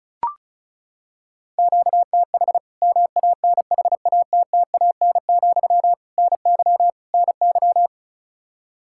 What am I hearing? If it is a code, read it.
Morse code, heard as a keyed tone, about 36 wpm, 700 Hz. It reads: QTH MANHATTAN, NY NY
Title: Practice QSO - Exchange 1 - Beginning - 35wpm